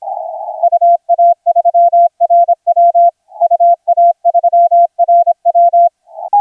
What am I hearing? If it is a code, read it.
Morse code, heard as a keyed tone, about 27 wpm, 690 Hz. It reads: UA3RW UA3RW
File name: RW3RN HI7 0455UTC 20m CW